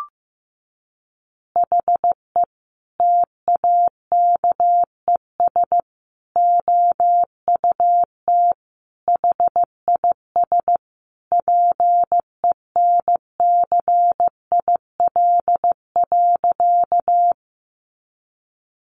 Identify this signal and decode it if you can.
Morse code, heard as a keyed tone, about 14 wpm, 700 Hz. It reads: HE TAKES OUT HIS PENCIL.
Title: Additional Easy Phrases 15wpm